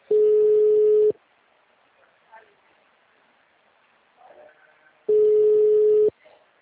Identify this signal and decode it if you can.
Ringback tone (Europe, 425 Hz), with speech.